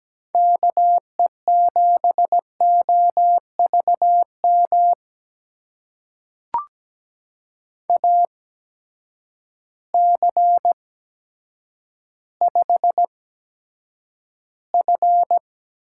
Morse code, heard as a keyed tone, about 17 wpm, spaced as for 9 wpm, 700 Hz. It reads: KE7OVM AC5F
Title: USA Call Signs - 8x Character Spacing - 17wpm